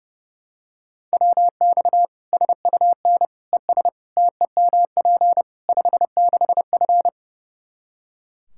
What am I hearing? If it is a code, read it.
Morse code, heard as a keyed tone, about 32 wpm, 700 Hz. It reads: WX SUN ES TEMP 56F
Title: Practice QSO - Exchange 2 - Info - 30wpm